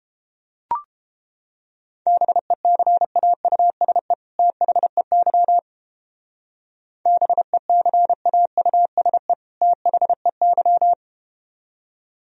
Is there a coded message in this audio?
Yes: Morse code from 2.1 to 10.9 s, heard as a keyed tone, about 31 wpm, 700 Hz. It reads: BECAUSE THEY BECAUSE THEY